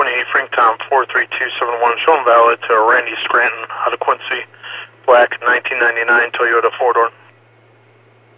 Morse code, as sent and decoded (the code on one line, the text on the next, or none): none